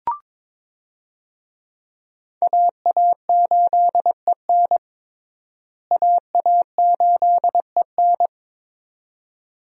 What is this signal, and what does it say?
Morse code, heard as a keyed tone, about 22 wpm, 700 Hz. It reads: AA8EN AA8EN
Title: USA Call Signs - 5 Characters - Repeated 2x - 22wpm